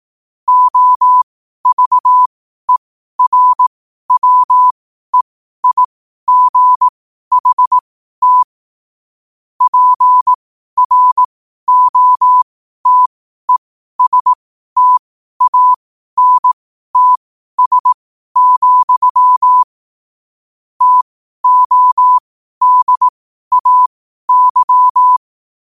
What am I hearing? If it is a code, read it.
Morse code, heard as a keyed tone, about 18 wpm, spaced as for 12 wpm, 1000 Hz. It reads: OVERWEIGHT PROTESTANTS, TODAY
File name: qotdfiles / 12wpm